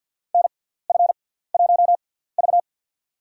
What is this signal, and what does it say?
Morse code, heard as a keyed tone, about 53 wpm, spaced as for 17 wpm, 700 Hz. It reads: NF1V
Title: USA Call Signs - 4 Characters - 6x Character Spacing - 50wpm